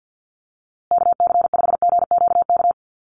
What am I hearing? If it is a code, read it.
Morse code, heard as a keyed tone, about 50 wpm, 700 Hz. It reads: KC5GQK
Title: USA Call Signs Repeated 2x 50wpm